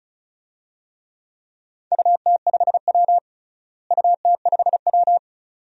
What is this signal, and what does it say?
Morse code, heard as a keyed tone, about 36 wpm, 700 Hz. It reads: UT5W UT5W
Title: DXCC-100 Call Signs - 4 Characters - Repeated 2x - 35wpm